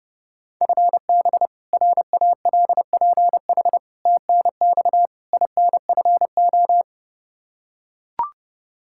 Morse code, heard as a keyed tone, about 30 wpm, 700 Hz. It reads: FB RALPH TNX INFO
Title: Practice QSO - Exchange 3 - More Info - 30wpm